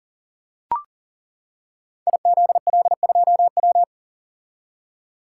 Morse code, heard as a keyed tone, about 39 wpm, 700 Hz. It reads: IZL2W
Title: Code Groups - 5 Letter:Numbers - 40wpm